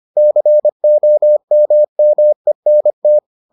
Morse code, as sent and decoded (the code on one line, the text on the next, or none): -.-. --- -- -- . -. -
COMMENT